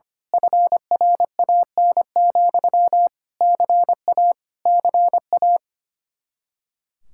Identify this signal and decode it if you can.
Morse code, heard as a keyed tone, about 25 wpm, 700 Hz. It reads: FRAN, CA CA
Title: Speed-Racing Common QSO Phrases Encore - 25wpm